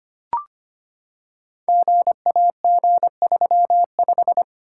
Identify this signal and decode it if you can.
Morse code, heard as a keyed tone, about 25 wpm, 700 Hz. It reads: GAG35
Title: Code Groups - 5 Letter:Numbers - 25wpm